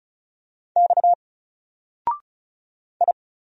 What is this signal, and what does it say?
Morse code, heard as a keyed tone, about 36 wpm, 700 Hz. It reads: X I
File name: Single-Letter-Number-35wpm